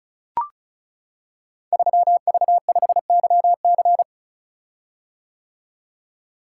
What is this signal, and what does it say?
Morse code, heard as a keyed tone, about 36 wpm, 700 Hz. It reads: 3V5YC
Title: Code Groups - 5 Letter:Numbers - 35wpm